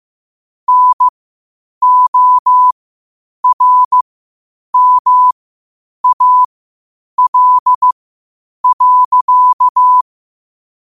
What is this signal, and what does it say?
Morse code, heard as a keyed tone, about 15 wpm, spaced as for 8 wpm, 1000 Hz. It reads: NORMAL.
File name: qotdfiles / 8wpm